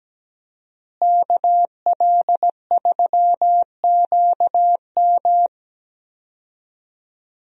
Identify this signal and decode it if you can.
Morse code, heard as a keyed tone, about 17 wpm, 700 Hz. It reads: KL3QM
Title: USA Call Signs - 5 Characters - Repeated 2x - 17wpm